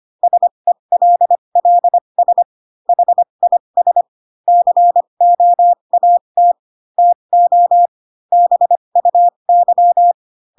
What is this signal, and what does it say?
Morse code, heard as a keyed tone, about 25 wpm, spaced as for 22 wpm, 700 Hz. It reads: SELLS HIS COAT TO BUY
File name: Pinocchio / Pinocchio ch8 18 25 ik1hge